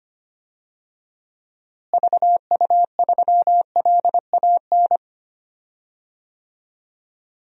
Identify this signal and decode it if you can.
Morse code, heard as a keyed tone, about 25 wpm, 700 Hz. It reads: VU3LAN
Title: DXCC-100 Call Signs - 6 Characters - Repeated 3x - 25wpm